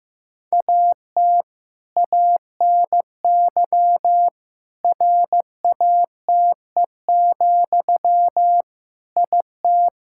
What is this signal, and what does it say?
Morse code, heard as a keyed tone, about 15 wpm, 700 Hz. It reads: AT ANY RATE, IT
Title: Sentences from Top 1600 Words 15wpm